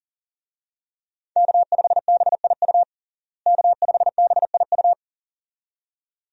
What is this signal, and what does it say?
Morse code, heard as a keyed tone, about 40 wpm, 700 Hz. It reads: K5BIU K5BIU
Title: USA Call Signs - 5 Characters - Repeated 2x - 40wpm